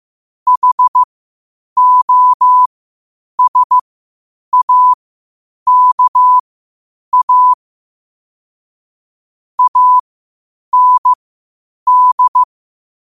Morse code, heard as a keyed tone, about 15 wpm, spaced as for 8 wpm, 1000 Hz. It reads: HOSAKA AND
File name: qotdfiles / 8wpm